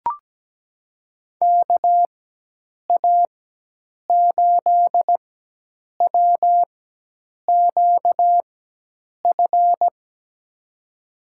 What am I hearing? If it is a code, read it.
Morse code, heard as a keyed tone, about 17 wpm, spaced as for 8 wpm, 700 Hz. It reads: KA8WQF
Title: USA Call Signs - 6 Characters - 4x Character Spacing - 17wpm